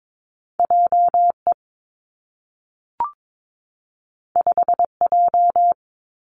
Morse code, heard as a keyed tone, about 22 wpm, 700 Hz. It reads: J E 5 J
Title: Two Challenging Letter-Number Combinations 22wpm